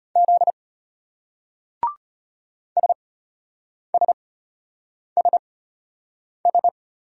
Morse code, heard as a keyed tone, about 34 wpm, spaced as for 8 wpm, 700 Hz. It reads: Z SSSS